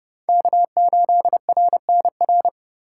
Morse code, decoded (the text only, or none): K8RNR